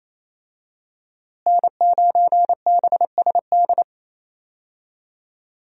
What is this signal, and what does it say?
Morse code, heard as a keyed tone, about 28 wpm, 700 Hz. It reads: N9BSD
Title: USA Call Signs - 5 Characters - Repeated 2x - 28wpm